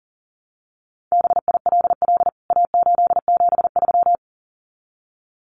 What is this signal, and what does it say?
Morse code, heard as a keyed tone, about 40 wpm, 700 Hz. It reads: BILL A873
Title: CWT Exchange 40wpm